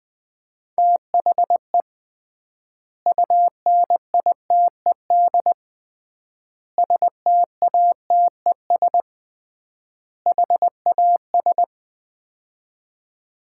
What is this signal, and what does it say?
Morse code, heard as a keyed tone, about 20 wpm, 700 Hz. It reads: THE UNITED STATES HAS
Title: 3x Word-Spacing - Sets of 4 Words - 20wpm